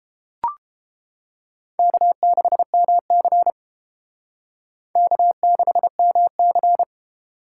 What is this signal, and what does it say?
Morse code, heard as a keyed tone, about 29 wpm, 700 Hz. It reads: K6MC K6MC